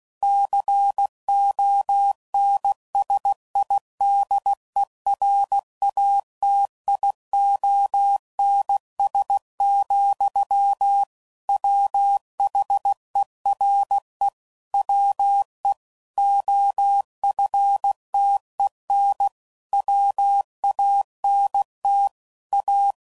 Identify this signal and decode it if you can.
Morse code, heard as a keyed tone, about 16 wpm, 800 Hz. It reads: CONSIDERATIONS, WHERE WE OFTEN WANT A
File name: CW Code Groups @15WPM / Technical Subjects / PwrSupply